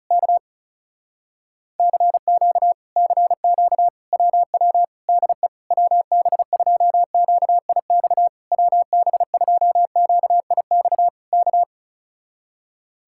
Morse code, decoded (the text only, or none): K CQ CQ WW DE WB2QIX WB2QIX K